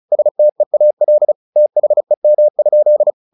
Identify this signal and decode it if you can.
Morse code, heard as a keyed tone, about 34 wpm, 600 Hz. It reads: STEAL THEM?
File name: the old man and the sea / the old man and the sea 1 35wpm